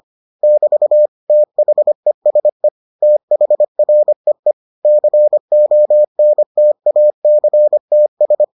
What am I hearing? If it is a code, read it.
Morse code, heard as a keyed tone, about 25 wpm, 600 Hz. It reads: <BT> THESE THREE CONTACTS